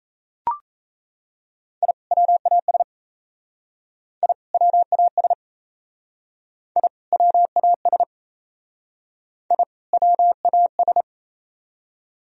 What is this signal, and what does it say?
Morse code, heard as a keyed tone, about 35 wpm, 700 Hz. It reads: I WAS I WAS I WAS I WAS